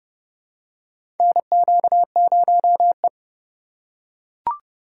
Morse code, heard as a keyed tone, about 29 wpm, 700 Hz. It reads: NQ0E
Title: USA Call Signs - 4 Characters - 2x Character Spacing - 30wpm